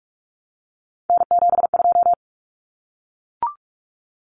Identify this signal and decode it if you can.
Morse code, heard as a keyed tone, about 44 wpm, 700 Hz. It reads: N72
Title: Common Cut Numbers - 3-Digit 45wpm